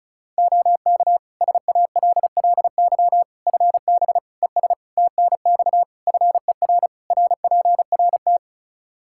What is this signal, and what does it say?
Morse code, heard as a keyed tone, about 36 wpm, 700 Hz. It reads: OK SALLY FB ES TNX FER RPRT
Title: Practice QSO - Exchange 2 - Info - 35wpm